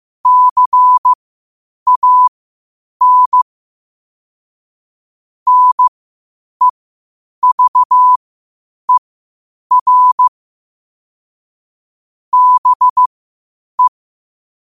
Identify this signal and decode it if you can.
Morse code, heard as a keyed tone, about 15 wpm, spaced as for 8 wpm, 1000 Hz. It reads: CAN NEVER BE
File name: qotdfiles / 8wpm